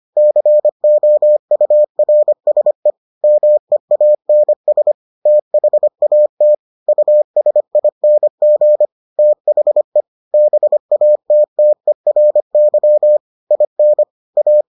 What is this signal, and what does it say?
Morse code, heard as a keyed tone, about 25 wpm, 600 Hz. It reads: COURSE MEANS THAT USING THE BATTERY IN A